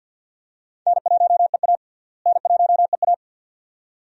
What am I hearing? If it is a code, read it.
Morse code, heard as a keyed tone, about 47 wpm, 700 Hz. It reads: N1EA N1EA
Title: USA Call Signs - 4 Characters - Repeated 2x - 50wpm